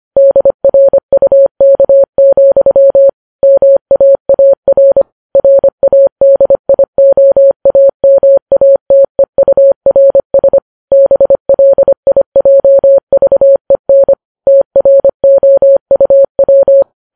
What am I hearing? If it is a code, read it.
Morse code, heard as a keyed tone, about 25 wpm, 550 Hz. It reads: DRUK, MAAR RADIOAMATEURS BLIJVEN TROUW
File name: Files Web tls / Morse on9fv / morse25kps